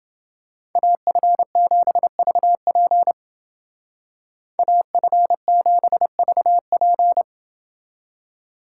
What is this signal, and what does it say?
Morse code, heard as a keyed tone, about 29 wpm, 700 Hz. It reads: AF7VP AF7VP